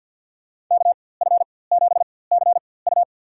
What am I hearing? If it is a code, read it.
Morse code, heard as a keyed tone, about 53 wpm, spaced as for 24 wpm, 700 Hz. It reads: KF7CU
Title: USA Call Signs - 5 Characters - 4x Character Spacing - 50wpm